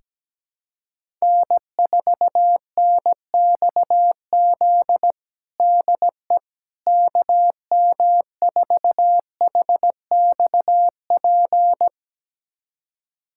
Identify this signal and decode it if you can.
Morse code, heard as a keyed tone, about 17 wpm, 700 Hz. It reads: N4NXZ DE KM4HXP
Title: USA Callsign DE Callsign 17wpm